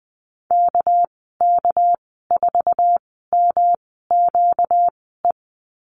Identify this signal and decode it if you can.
Morse code, heard as a keyed tone, about 19 wpm, spaced as for 14 wpm, 700 Hz. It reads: KK4MQE
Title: 245 Lesson 39g - Letter z - Call Signs 20wpm